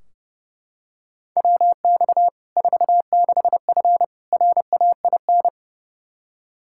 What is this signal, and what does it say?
Morse code, heard as a keyed tone, about 30 wpm, 700 Hz. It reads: WX 46F RAIN
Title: Practice QSO - Exchange 2 - Info - 30wpm